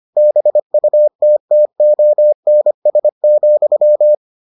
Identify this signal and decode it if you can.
Morse code, heard as a keyed tone, about 25 wpm, 600 Hz. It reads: BUTTONS,